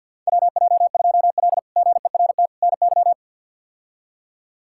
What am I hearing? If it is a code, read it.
Morse code, heard as a keyed tone, about 50 wpm, 700 Hz. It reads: WJ2L GERT NY
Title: K1USN SST QSO Practice - 50wpm